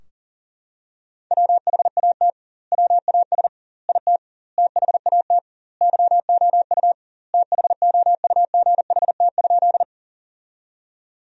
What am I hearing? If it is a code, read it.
Morse code, heard as a keyed tone, about 40 wpm, 700 Hz. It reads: WHAT WAS IT THAT YOU THOUGHT?
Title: 2x Word-Spacing - Sentences from Top 300 Words - 40wpm